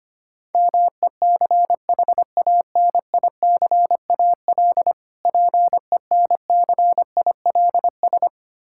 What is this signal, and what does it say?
Morse code, heard as a keyed tone, about 25 wpm, 700 Hz. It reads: MECHANICAL PENCILS
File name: practice / 25wpm / Difficult Words with Hints - 25wpm